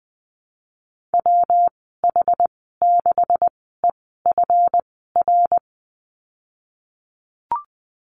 Morse code, heard as a keyed tone, about 20 wpm, spaced as for 14 wpm, 700 Hz. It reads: WH6EFR
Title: 161 Lesson 26g - Letter g - Call Signs 20wpm